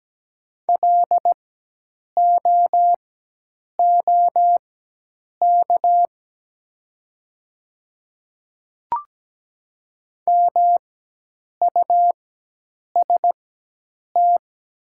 Morse code, heard as a keyed tone, about 17 wpm, spaced as for 8 wpm, 700 Hz. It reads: LOOK MUST
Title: 4x Character Spacing - Top 300 Words - 17wpm